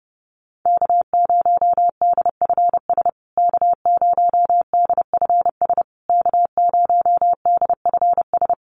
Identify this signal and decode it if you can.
Morse code, heard as a keyed tone, about 30 wpm, 700 Hz. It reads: K0DFS K0DFS K0DFS
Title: USA Call Signs Repeated 3x 30wpm